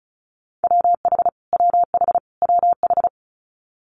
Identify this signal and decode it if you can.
Morse code, heard as a keyed tone, about 36 wpm, 700 Hz. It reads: WH WH WH
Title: Multi-letter Phonemes Repeated 3x 35wpm